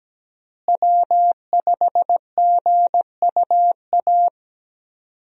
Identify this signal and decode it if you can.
Morse code, heard as a keyed tone, about 17 wpm, 700 Hz. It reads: W5GUA
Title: USA Call Signs - 5 Characters - Repeated 2x - 17wpm